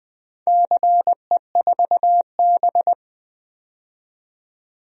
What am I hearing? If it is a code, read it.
Morse code, heard as a keyed tone, about 20 wpm, 700 Hz. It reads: CE4B
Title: DXCC-100 Call Signs - 4 Characters - 20wpm